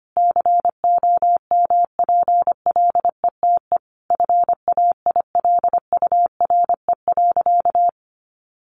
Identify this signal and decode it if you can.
Morse code, heard as a keyed tone, about 25 wpm, 700 Hz. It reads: COMPLETE FAILURE.
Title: Sentences from Top 1400 Words 25wpm